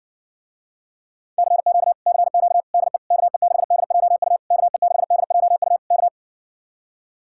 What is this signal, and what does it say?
Morse code, heard as a keyed tone, about 66 wpm, 700 Hz. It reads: CQ CQ DE KE6DJU KE6DJU K